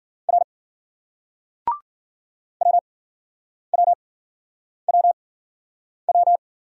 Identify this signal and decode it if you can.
Morse code, heard as a keyed tone, about 51 wpm, spaced as for 9 wpm, 700 Hz. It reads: R WWWW